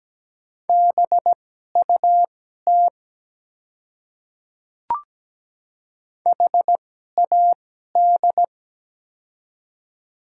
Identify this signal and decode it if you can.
Morse code, heard as a keyed tone, about 17 wpm, spaced as for 12 wpm, 700 Hz. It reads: BUT HAD
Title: 2x Character Spacing - Top 100 Words - 17wpm